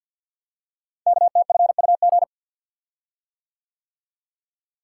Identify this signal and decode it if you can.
Morse code, heard as a keyed tone, about 49 wpm, 700 Hz. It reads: KTFUG
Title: Code Groups - 5 Letters - 50wpm